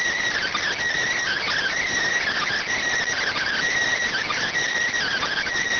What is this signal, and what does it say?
Emergency Alert System (SAME) tones (header bursts), heard as harsh buzzing bursts. It is heard from 1.2 to 5.5 s.